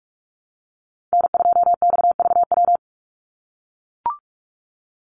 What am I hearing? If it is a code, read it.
Morse code, heard as a keyed tone, about 44 wpm, 700 Hz. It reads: N2XVW
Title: USA Call Signs 45wpm